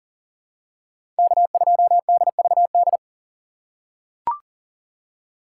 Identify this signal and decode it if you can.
Morse code, heard as a keyed tone, about 40 wpm, 700 Hz. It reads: K2DVD
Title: USA Call Signs - 5 Characters - 2x Character Spacing - 40wpm